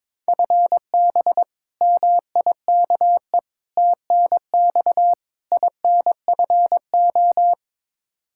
Morse code, heard as a keyed tone, about 22 wpm, 700 Hz. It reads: FB MIKE TNX INFO
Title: Practice QSO - Exchange 3 - More Info - 22wpm